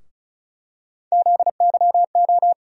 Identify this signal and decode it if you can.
Morse code, heard as a keyed tone, about 36 wpm, 700 Hz. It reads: ZYO
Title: USA Call Sign Postfixes - 3 Letters 35wpm